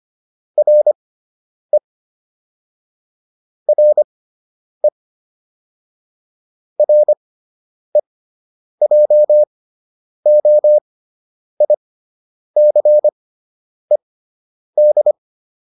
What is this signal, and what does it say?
Morse code, heard as a keyed tone, about 25 wpm, spaced as for 9 wpm, 600 Hz. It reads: RE RE REJOICED